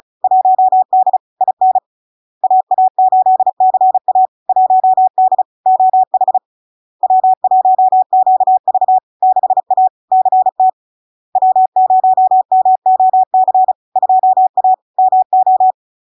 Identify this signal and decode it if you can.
Morse code, heard as a keyed tone, about 35 wpm, 750 Hz. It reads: 1D IN AA8CA 1D OH W1QV 6A CT W0MOC 2A MO
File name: LARK-FD-2022-Log-in-CW-35wpm-750-Hz